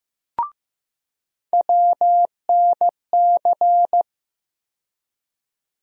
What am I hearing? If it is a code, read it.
Morse code, heard as a keyed tone, about 15 wpm, 700 Hz. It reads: WNC
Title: USA Call Sign Postfixes - 3 Letters 15wpm